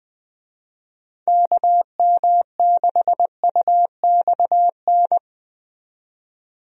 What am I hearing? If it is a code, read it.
Morse code, heard as a keyed tone, about 20 wpm, 700 Hz. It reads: KM6UXN
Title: USA Call Signs - 6 Characters - 20wpm